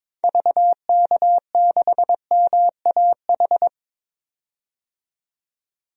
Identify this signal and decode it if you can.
Morse code, heard as a keyed tone, about 21 wpm, 700 Hz. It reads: VK6MAH
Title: DXCC-100 Call Signs - 6 Characters - 22wpm